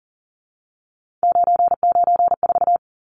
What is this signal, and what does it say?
Morse code, heard as a keyed tone, about 40 wpm, 700 Hz. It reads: 994
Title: Numbers - 3-Digit 40wpm